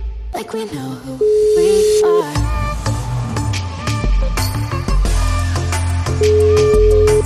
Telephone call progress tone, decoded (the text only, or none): ringback tone